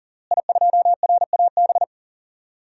Morse code, heard as a keyed tone, about 40 wpm, 700 Hz. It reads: I2RAB
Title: DXCC-100 Call Signs - 5 Characters - Repeated 3x - 40wpm